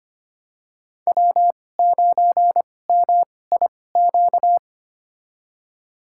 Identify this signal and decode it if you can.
Morse code, heard as a keyed tone, about 25 wpm, spaced as for 18 wpm, 700 Hz. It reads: W9MIQ
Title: USA Call Signs - 2x Character Spacing - 25wpm